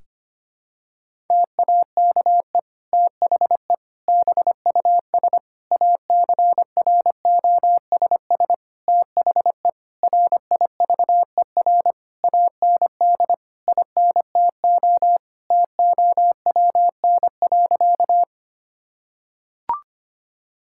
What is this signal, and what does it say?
Morse code, heard as a keyed tone, about 25 wpm, 700 Hz. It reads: TAKE THE BUS ACROSS THE RIVER AND INTO TOWN.
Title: Sentences from Top 1000 Words - Review Part 6 - 25wpm